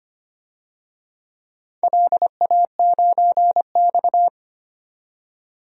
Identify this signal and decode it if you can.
Morse code, heard as a keyed tone, about 25 wpm, 700 Hz. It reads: LA9X
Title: DXCC-100 Call Signs - 4 Characters - 25wpm